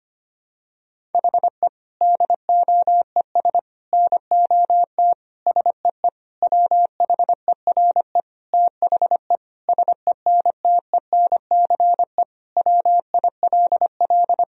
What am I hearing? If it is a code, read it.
Morse code, heard as a keyed tone, about 25 wpm, 700 Hz. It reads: HE DOES NOT SEE WHERE THE SENTENCE WILL
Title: Sentences from Top 1000 Words - Review Part 1 - 25wpm